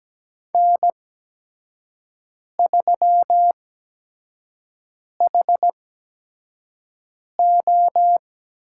Morse code, heard as a keyed tone, about 17 wpm, spaced as for 5 wpm, 700 Hz. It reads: N3HO
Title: USA Call Signs - 4 Characters - 8x Character Spacing - 17wpm